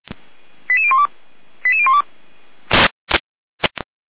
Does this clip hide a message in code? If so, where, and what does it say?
Yes: five-tone selective call (ZVEI) from 0.7 to 2.0 s, heard as quick runs of five tones, tones of about 70 ms. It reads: 88012 88012